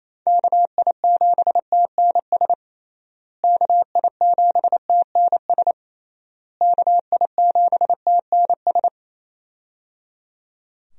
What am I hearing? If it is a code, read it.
Morse code, heard as a keyed tone, about 28 wpm, 700 Hz. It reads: KI7TNS KI7TNS KI7TNS
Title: USA Call Signs - 6 Characters - Repeated 3x - 28wpm